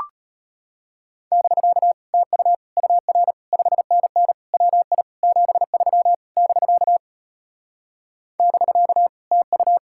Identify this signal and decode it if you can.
Morse code, heard as a keyed tone, about 33 wpm, 700 Hz. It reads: <BK> TU UR 5NN WI 73 <BK> <BK> TU